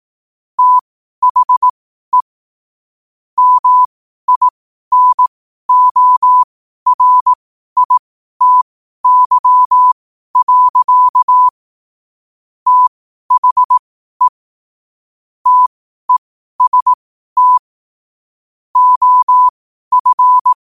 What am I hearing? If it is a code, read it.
Morse code, heard as a keyed tone, about 18 wpm, spaced as for 12 wpm, 1000 Hz. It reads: THE MINORITY. THE TEST OF
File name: qotdfiles / 12wpm